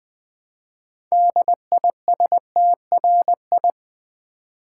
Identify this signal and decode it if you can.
Morse code, heard as a keyed tone, about 20 wpm, 700 Hz. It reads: DISTRI
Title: Build A Word - 2 Letters At A Time - 8-Letter Words - 20wpm